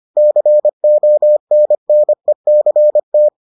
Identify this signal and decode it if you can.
Morse code, heard as a keyed tone, about 25 wpm, 600 Hz. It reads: CONNECT